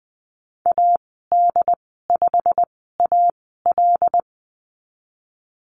Morse code, heard as a keyed tone, about 20 wpm, spaced as for 14 wpm, 700 Hz. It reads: AD5AL
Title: 088 Lesson 15g - Number 5 - Call Signs 20wpm